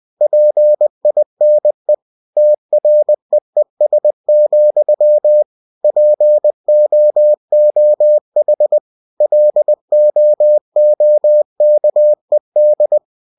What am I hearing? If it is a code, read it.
Morse code, heard as a keyed tone, about 20 wpm, 600 Hz. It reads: PINE TREES, POOH LOOKED
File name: cw-books / winnie-the-pooh / winnie-20wpm-page-0034